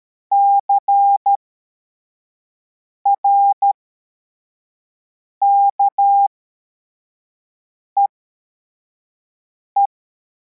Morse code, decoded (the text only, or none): CRKEE